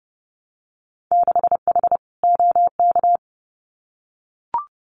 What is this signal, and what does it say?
Morse code, heard as a keyed tone, about 30 wpm, 700 Hz. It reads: 6H OK
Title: Winter Field Day Exchanges Encore 30wpm